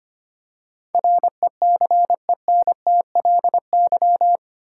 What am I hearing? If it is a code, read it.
Morse code, heard as a keyed tone, about 25 wpm, 700 Hz. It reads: RECENTLY
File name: Eight Letter Words - 25wpm